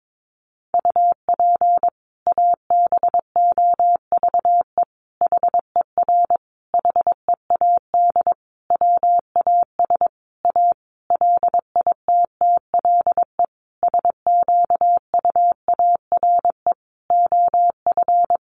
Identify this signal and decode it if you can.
Morse code, heard as a keyed tone, about 22 wpm, 700 Hz. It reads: UP ABOVE HER HEAD WAS A LITTLE SQUARE OF
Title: Sentences from Top 1200 Words 22wpm